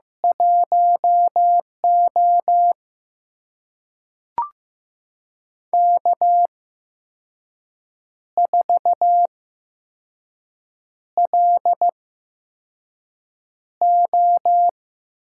Morse code, heard as a keyed tone, about 15 wpm, spaced as for 5 wpm, 700 Hz. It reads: 1O K4LO
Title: USA Call Signs - 4 Characters - 8x Character Spacing - 15wpm